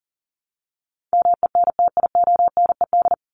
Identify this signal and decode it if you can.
Morse code, heard as a keyed tone, about 40 wpm, 700 Hz. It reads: MENTIONED
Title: ED Suffix - Top 100 Words with Base Words 40wpm